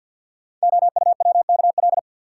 Morse code, heard as a keyed tone, about 50 wpm, 700 Hz. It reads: OUWKL
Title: Code Groups - 5 Letters - 50wpm